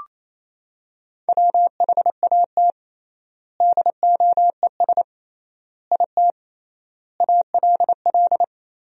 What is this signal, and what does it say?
Morse code, heard as a keyed tone, about 28 wpm, 700 Hz. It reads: WHAT DOES IT ALL
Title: 3x Word-Spacing - Sentences from Top 200 Words - 28wpm